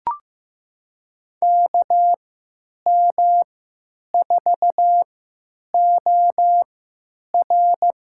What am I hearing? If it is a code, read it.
Morse code, heard as a keyed tone, about 15 wpm, spaced as for 8 wpm, 700 Hz. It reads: KM4OR
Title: USA Call Signs - 3x Character Spacing - 15wpm